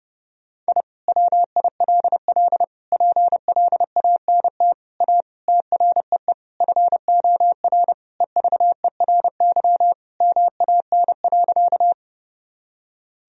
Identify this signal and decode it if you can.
Morse code, heard as a keyed tone, about 30 wpm, 700 Hz. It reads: I WILL PLANT A TREE FOR EVERY MAN.
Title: Sentences from Top 1000 Words - Review Part 1 - 30wpm